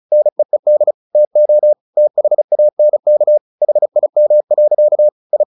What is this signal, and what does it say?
Morse code, heard as a keyed tone, about 34 wpm, 600 Hz. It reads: NEED TO THANK HIM. I
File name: the old man and the sea / the old man and the sea 1 35wpm